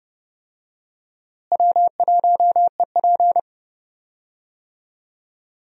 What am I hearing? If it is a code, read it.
Morse code, heard as a keyed tone, about 30 wpm, 700 Hz. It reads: W1EP
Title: USA Call Signs - 4 Characters - Repeated 3x - 30wpm